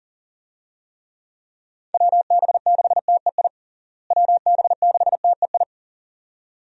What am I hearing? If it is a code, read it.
Morse code, heard as a keyed tone, about 40 wpm, 700 Hz. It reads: WB6TEI WB6TEI
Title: USA Call Signs - 6 Characters - Repeated 2x - 40wpm